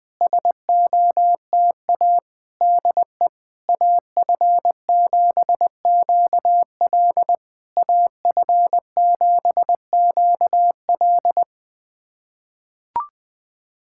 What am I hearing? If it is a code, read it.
Morse code, heard as a keyed tone, about 20 wpm, 700 Hz. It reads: SOTA DE AF7QL AF7QL
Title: SOTA QSO Practice 20wpm